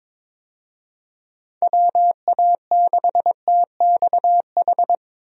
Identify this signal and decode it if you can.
Morse code, heard as a keyed tone, about 22 wpm, 700 Hz. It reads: WA6TXH
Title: USA Call Signs - 6 Characters - Repeated 2x - 22wpm